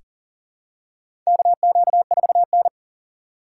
Morse code, heard as a keyed tone, about 40 wpm, 700 Hz. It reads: KQ4N
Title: USA Call Signs - 4 Characters - 6x Character Spacing - 40wpm